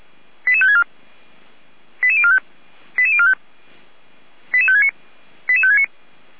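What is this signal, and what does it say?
Five-tone selective call (ZVEI), heard as quick runs of five tones, tones of about 80 ms. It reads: 88564 88045 88045 88569 88569
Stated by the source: Received by SDR; ZVEI five-tone call.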